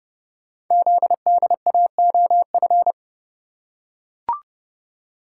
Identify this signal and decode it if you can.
Morse code, heard as a keyed tone, about 28 wpm, 700 Hz. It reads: ZDAOF
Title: Code Groups - 5 Letter:Numbers - 30wpm